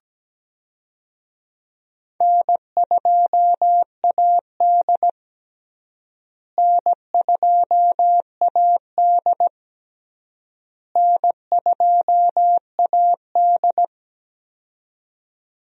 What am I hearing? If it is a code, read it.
Morse code, heard as a keyed tone, about 17 wpm, 700 Hz. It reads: N2AD N2AD N2AD
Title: USA Call Signs - 4 Characters - Repeated 3x - 17wpm